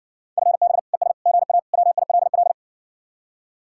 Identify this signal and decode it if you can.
Morse code, heard as a keyed tone, about 60 wpm, 700 Hz. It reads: FB ES GA WILL